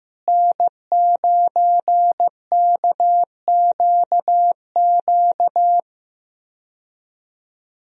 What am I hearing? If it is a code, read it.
Morse code, heard as a keyed tone, about 15 wpm, 700 Hz. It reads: N9KQQ
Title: USA Call Signs - 5 Characters - Repeated 2x - 15wpm